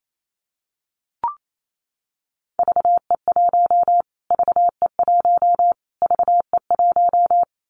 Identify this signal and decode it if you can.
Morse code, heard as a keyed tone, about 28 wpm, 700 Hz. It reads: VE1 VE1 VE1
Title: Callsign Prefixes for Top 25 DXCC - Repeated 3x - 28wpm